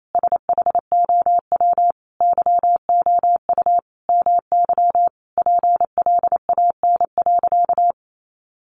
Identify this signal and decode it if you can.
Morse code, heard as a keyed tone, about 28 wpm, 700 Hz. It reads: SHOW YOU MY PLAN.
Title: Sentences from Top 500 Words 28wpm